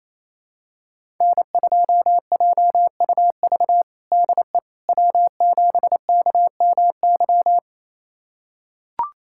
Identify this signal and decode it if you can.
Morse code, heard as a keyed tone, about 27 wpm, 700 Hz. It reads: N2JUV DE W7KMY
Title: USA Callsign DE Callsign - 5 Characters - 28wpm